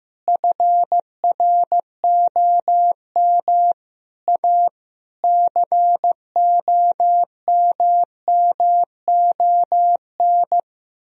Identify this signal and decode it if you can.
Morse code, heard as a keyed tone, about 15 wpm, 700 Hz. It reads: FROM A COMMON
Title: Sentences from Top 1000 Words - Review Part 4 - 15wpm